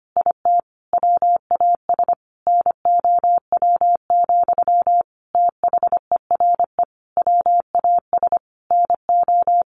Morse code, heard as a keyed tone, about 25 wpm, 700 Hz. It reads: IT WAS NOW, THERE WAS NO
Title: A Sound Of Thunder 25wpm